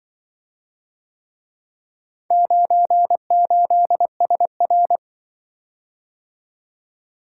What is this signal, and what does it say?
Morse code, heard as a keyed tone, about 24 wpm, 700 Hz. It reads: 98SR
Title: Speed-Racing - Four Letter-Number Combinations - 20wpm